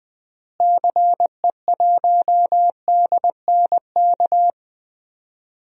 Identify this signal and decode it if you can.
Morse code, heard as a keyed tone, about 20 wpm, 700 Hz. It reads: CE1DNK
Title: DXCC-100 Call Signs - 6 Characters - Repeated 3x - 20wpm